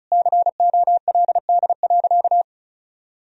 Morse code, heard as a keyed tone, about 34 wpm, 700 Hz. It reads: COLD.
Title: Sentences from Top 1000 Words - Review Part 4 - 35wpm